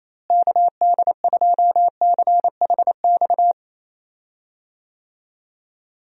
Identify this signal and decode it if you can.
Morse code, heard as a keyed tone, about 28 wpm, 700 Hz. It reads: KD2CHX
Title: USA Call Signs - 6 Characters - Repeated 2x - 28wpm